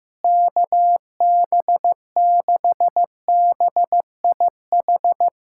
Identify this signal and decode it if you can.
Morse code, heard as a keyed tone, about 15 wpm, 700 Hz. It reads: KB6BIH
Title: USA Callsign DE Callsign - 6 Characters - 15wpm